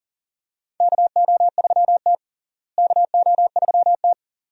Morse code, heard as a keyed tone, about 37 wpm, 700 Hz. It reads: KO3T KO3T
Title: USA Call Signs - 4 Characters - Repeated 2x - 40wpm